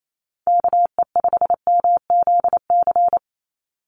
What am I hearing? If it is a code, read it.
Morse code, heard as a keyed tone, about 28 wpm, 700 Hz. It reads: KE5MZC
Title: USA Call Signs 28wpm